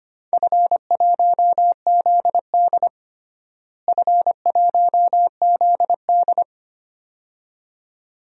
Morse code, heard as a keyed tone, about 25 wpm, 700 Hz. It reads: F1ZD F1ZD
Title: DXCC-100 Call Signs - 4 Characters - Repeated 2x - 25wpm